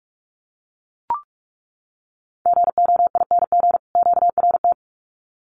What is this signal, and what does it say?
Morse code, heard as a keyed tone, about 44 wpm, 700 Hz. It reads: GOING QRT
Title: Common QSO Phrases 45wpm